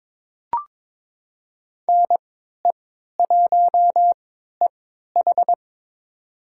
Morse code, heard as a keyed tone, about 22 wpm, spaced as for 12 wpm, 700 Hz. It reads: NE1EH
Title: USA Call Signs - 3x Character Spacing - 22wpm